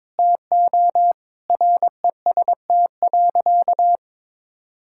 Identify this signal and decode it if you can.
Morse code, heard as a keyed tone, about 22 wpm, 700 Hz. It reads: TO REST.
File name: practice / 22wpm / Sentences with 8-Word Phrases - 22wpm